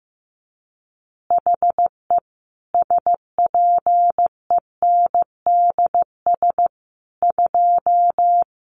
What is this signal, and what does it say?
Morse code, heard as a keyed tone, about 15 wpm, 700 Hz. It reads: HE SPENDS 2
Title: Additional Easy Phrases 15wpm